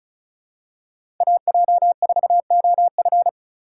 Morse code, heard as a keyed tone, about 36 wpm, 700 Hz. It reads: AJ4OF
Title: USA Call Signs - 5 Characters - 3x Character Spacing - 35wpm